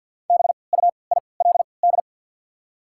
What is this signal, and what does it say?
Morse code, heard as a keyed tone, about 54 wpm, 690 Hz. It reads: B U I L D
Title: 3x Character Spacing - Top 300 Words - 50wpm